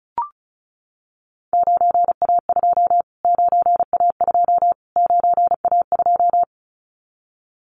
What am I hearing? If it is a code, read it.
Morse code, heard as a keyed tone, about 34 wpm, 700 Hz. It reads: 9A2 9A2 9A2
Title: Callsign Prefixes for Top 25 DXCC - Repeated 3x - 35wpm